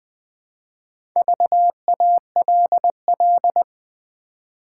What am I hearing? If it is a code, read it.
Morse code, heard as a keyed tone, about 20 wpm, 700 Hz. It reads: VALL
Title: Build A Word - 2 Letters At A Time - 6-Letter Words - 20wpm